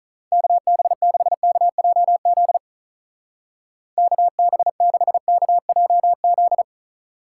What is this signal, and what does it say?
Morse code, heard as a keyed tone, about 44 wpm, 700 Hz. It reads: KB6KJZ KB6KJZ